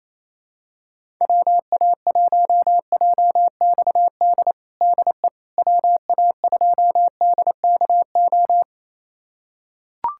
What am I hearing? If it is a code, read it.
Morse code, heard as a keyed tone, about 27 wpm, 700 Hz. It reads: WA1JXD DE WA2DKO
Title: USA Callsign DE Callsign - 6 Characters - 28wpm